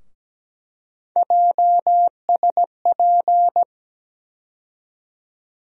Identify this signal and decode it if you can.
Morse code, heard as a keyed tone, about 17 wpm, 700 Hz. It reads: JSP
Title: USA Call Sign Postfixes - 3 Letters 17wpm